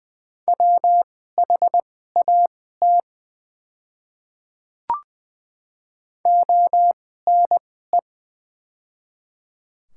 Morse code, heard as a keyed tone, about 20 wpm, spaced as for 15 wpm, 700 Hz. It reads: WHAT ONE
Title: 2x Character Spacing - Top 100 Words - 20wpm